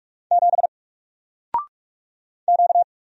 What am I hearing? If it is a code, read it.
Morse code, heard as a keyed tone, about 45 wpm, 700 Hz. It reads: 7 Q
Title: Single Letter-Number Alt 45wpm